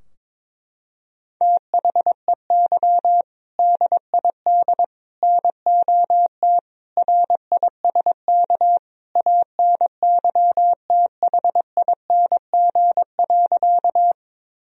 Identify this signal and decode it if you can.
Morse code, heard as a keyed tone, about 21 wpm, 700 Hz. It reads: THEY DID NOT RISK ANYTHING.
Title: Sentences from Top 1000 Words - Review Part 4 - 22wpm